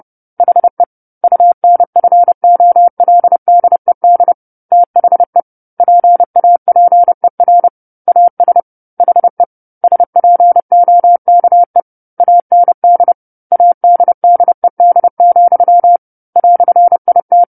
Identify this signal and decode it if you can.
Morse code, heard as a keyed tone, about 30 wpm, 700 Hz. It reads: HE UNFOLDED THE PAPER AS HE SPOKE AND ADDED, "IT
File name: 10-Chapter-10 30 WPM